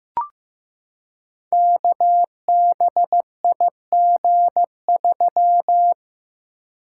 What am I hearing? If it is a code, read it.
Morse code, heard as a keyed tone, about 15 wpm, 700 Hz. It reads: KBIG3
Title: Code Groups - 5 Letter:Numbers - 15wpm